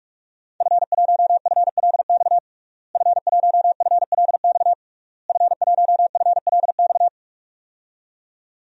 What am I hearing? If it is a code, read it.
Morse code, heard as a keyed tone, about 41 wpm, 700 Hz. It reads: F1FLX F1FLX F1FLX
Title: DXCC-100 Call Signs - 5 Characters - Repeated 3x - 45wpm